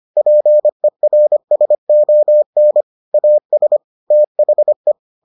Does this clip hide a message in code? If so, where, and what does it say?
Yes: Morse code from 0.2 to 4.9 s, heard as a keyed tone, about 25 wpm, 600 Hz. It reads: PERSON AS THE